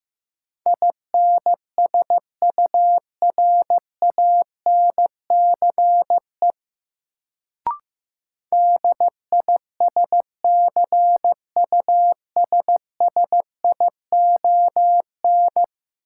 Morse code, heard as a keyed tone, about 15 wpm, 700 Hz. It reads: INSURANCE DISCUSSION
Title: Sentences from Top 1100 Words - New Word Drill - 15wpm